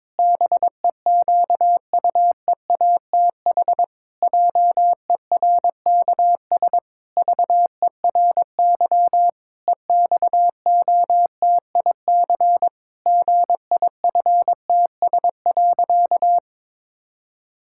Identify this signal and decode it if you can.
Morse code, heard as a keyed tone, about 22 wpm, 700 Hz. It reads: BEQUEATH JERKS VERY EXOTIC GIFTS.
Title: Pangram Challenge v2 - 22wpm